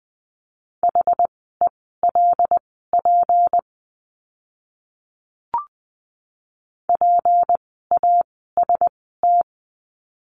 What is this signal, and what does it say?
Morse code, heard as a keyed tone, about 20 wpm, spaced as for 14 wpm, 700 Hz. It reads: HELP PAST
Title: 153 Lesson 25f - Letter p - Words 20wpm